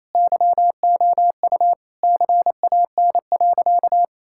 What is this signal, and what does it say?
Morse code, heard as a keyed tone, about 27 wpm, 700 Hz. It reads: YOU CAN.
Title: Sentences from Top 1600 Words 28wpm